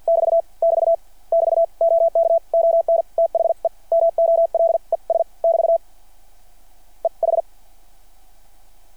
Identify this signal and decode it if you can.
Morse code, heard as a keyed tone, about 50 wpm, spaced as for 43 wpm, 660 Hz. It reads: <BT> <BT> <BT> OK ON THE MOLES <BT> E H
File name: archived / CRSnet